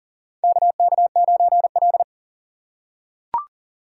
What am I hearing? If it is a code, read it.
Morse code, heard as a keyed tone, about 40 wpm, 700 Hz. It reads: KK9L
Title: USA Call Signs - 4 Characters - 2x Character Spacing - 40wpm